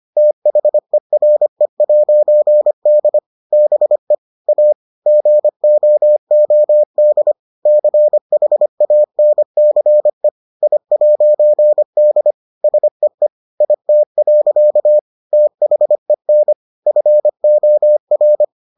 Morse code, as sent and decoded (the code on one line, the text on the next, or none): - .... .-. .----. -.. / -... / .- / --. --- --- -.. / -.-. .... .- -. -.-. / .. .----. -.. / ... . / .. - .-.-.- / - .... -. / ..-. --- .-.
THERE'D BE A GOOD CHANCE I'D SEE IT. THEN FOR